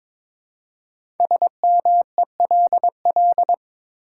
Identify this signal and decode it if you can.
Morse code, heard as a keyed tone, about 23 wpm, 700 Hz. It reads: SMELL
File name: 5 Letter Words - First 3 Letters Given - 22wpm